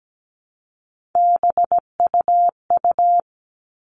Morse code, heard as a keyed tone, about 17 wpm, 700 Hz. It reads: BUU
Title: All Cut Numbers - 3-Digit 17wpm